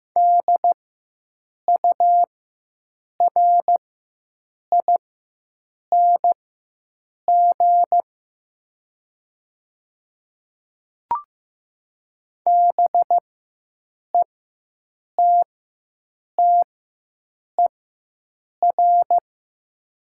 Morse code, heard as a keyed tone, about 15 wpm, spaced as for 7 wpm, 700 Hz. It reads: DURING BETTER
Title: 4x Character Spacing - Top 500 Words - 15wpm